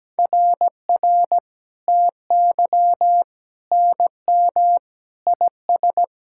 Morse code, heard as a keyed tone, about 17 wpm, 700 Hz. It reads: RR TY NM IS
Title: SOTA QSO Practice 17wpm